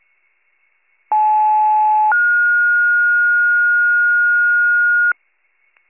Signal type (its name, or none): two-tone page